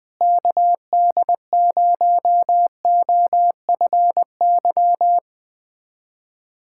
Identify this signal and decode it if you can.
Morse code, heard as a keyed tone, about 21 wpm, 700 Hz. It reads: KD0OFY
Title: USA Callsign DE Callsign 20wpm